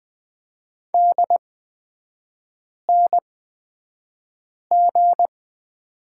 Morse code, heard as a keyed tone, about 20 wpm, spaced as for 5 wpm, 700 Hz. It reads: DNG
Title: 236 Lesson 38e - Letter q - Three Characters 20wpm